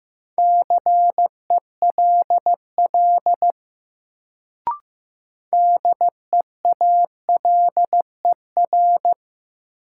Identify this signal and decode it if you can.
Morse code, heard as a keyed tone, about 15 wpm, 700 Hz. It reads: CELL DEALER
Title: Sentences from Top 1100 Words - New Word Drill - 15wpm